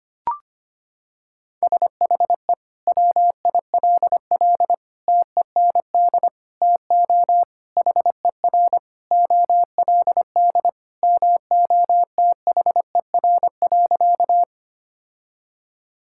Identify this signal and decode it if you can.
Morse code, heard as a keyed tone, about 25 wpm, 700 Hz. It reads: SHE WILL TEND TO HER OLD MOTHER.
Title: Sentences from Top 1000 Words - Review Part 5 - 25wpm